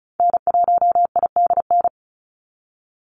Morse code, heard as a keyed tone, about 34 wpm, 700 Hz. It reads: N1IDN
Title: NA SOTA Activator Callsigns 35wpm